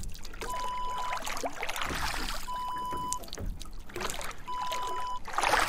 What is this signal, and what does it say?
There is no code.